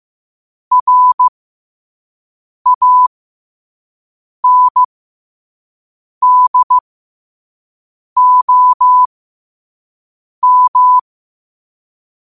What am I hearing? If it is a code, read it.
Morse code, heard as a keyed tone, about 15 wpm, spaced as for 5 wpm, 1000 Hz. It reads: RANDOM
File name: qotdfiles / 5wpm